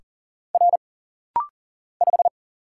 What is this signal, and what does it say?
Morse code, heard as a keyed tone, about 39 wpm, 700 Hz. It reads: R 5
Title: Single Letter-Number Alt - Rapid-Fire 40wpm